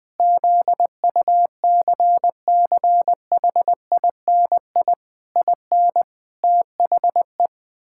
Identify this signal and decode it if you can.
Morse code, heard as a keyed tone, about 20 wpm, 700 Hz. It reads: ZUCCHINI IN THE
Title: Pangram Challenge v2 - 20wpm